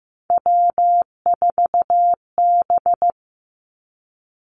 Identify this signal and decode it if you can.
Morse code, heard as a keyed tone, about 15 wpm, 700 Hz. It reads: W4B
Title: Three Challenging Letter-Number Combinations 15wpm